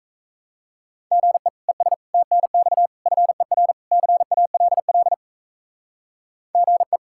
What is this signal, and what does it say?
Morse code, heard as a keyed tone, about 34 wpm, 700 Hz. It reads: GE ES TNX FER CALL GE
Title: Speed-Racing Common QSO Phrases - 28wpm